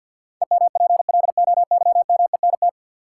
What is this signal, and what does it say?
Morse code, heard as a keyed tone, about 47 wpm, 700 Hz. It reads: EMPLOYMENT
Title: Sentences from Top 1500 Words - New Word Drill - 50wpm